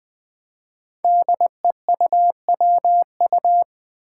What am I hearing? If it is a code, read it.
Morse code, heard as a keyed tone, about 20 wpm, 700 Hz. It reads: DEUWU
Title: Five Challenging Letter-Number Combinations 20wpm